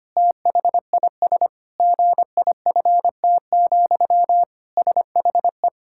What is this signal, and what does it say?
Morse code, heard as a keyed tone, about 25 wpm, 700 Hz. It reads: THIS GIFT, SHE
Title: Sentences from Top 1000 Words - Review Part 6 - 25wpm